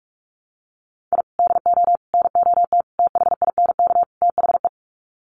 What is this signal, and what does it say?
Morse code, heard as a keyed tone, about 44 wpm, 700 Hz. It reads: I DO NOT THINK THE
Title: Sets of 5 Words 45wpm